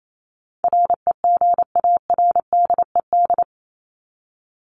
Morse code, heard as a keyed tone, about 28 wpm, 700 Hz. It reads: REGARDED
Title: ED Suffix - Top 100 Words with Base Words 28wpm